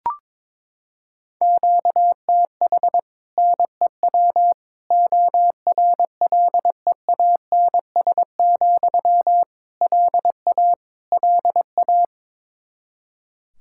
Morse code, heard as a keyed tone, about 22 wpm, 700 Hz. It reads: QTH NEW ORLEANS, LA LA
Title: Practice QSO - Exchange 1 - Beginning - 22wpm